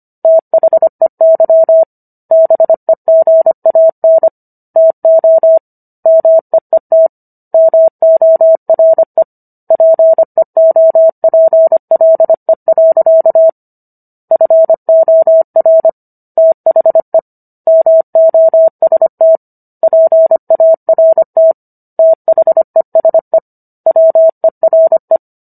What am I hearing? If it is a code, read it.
Morse code, heard as a keyed tone, about 25 wpm, 650 Hz. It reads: THEY BEGAN TO MEET MORE PEOPLE. FOR THE MOST PART THESE WERE